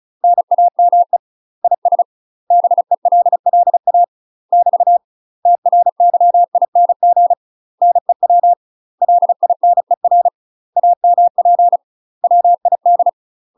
Morse code, heard as a keyed tone, about 36 wpm, 700 Hz. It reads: NAME IS BELLA <BT> TRYING NEW LINER AMP WID
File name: UK MASTER QSOs TRAINING / 35WPM / UK MASTER QSOs 0027